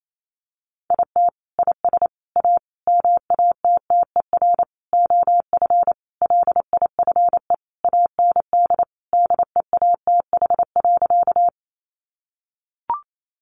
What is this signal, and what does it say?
Morse code, heard as a keyed tone, about 28 wpm, 700 Hz. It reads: IT IS A MATTER OF LIFE AND DEATH.
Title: Sentences from Top 1100 Words 28wpm